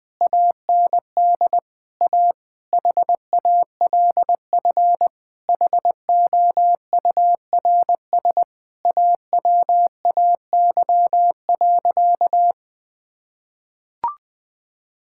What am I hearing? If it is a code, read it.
Morse code, heard as a keyed tone, about 21 wpm, 700 Hz. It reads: AND A HALF HOURS AWAY.
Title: Sentences with 4-Word Phrases - 20wpm